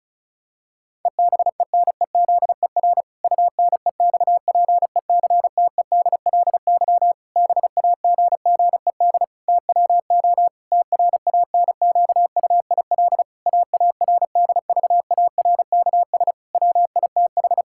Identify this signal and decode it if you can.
Morse code, heard as a keyed tone, about 34 wpm, 700 Hz. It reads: EBENEZER UNEXPECTEDLY BAGGED TWO TRANQUIL AARDVARKS WITH
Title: Pangram Challenge v2 - 35wpm